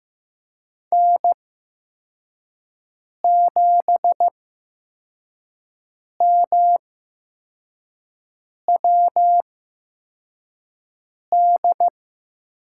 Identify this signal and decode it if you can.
Morse code, heard as a keyed tone, about 15 wpm, spaced as for 4 wpm, 700 Hz. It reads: N7MWD
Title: USA Call Signs - 8x Character Spacing - 15wpm